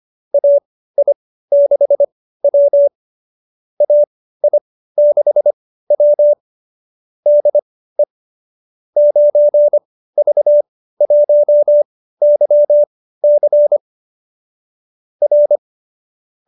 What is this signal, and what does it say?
Morse code, heard as a keyed tone, about 25 wpm, spaced as for 15 wpm, 570 Hz. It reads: AI6W AI6W DE 9V1YC R